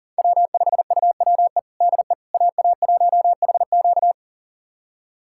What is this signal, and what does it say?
Morse code, heard as a keyed tone, about 40 wpm, 700 Hz. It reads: W5UWE DE AA1HQ
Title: USA Callsign DE Callsign - 5 Characters - 40wpm